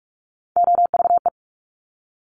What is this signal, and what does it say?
Morse code, heard as a keyed tone, about 46 wpm, 700 Hz. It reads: OVE
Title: Top Three Letter Combinations in Words 45wpm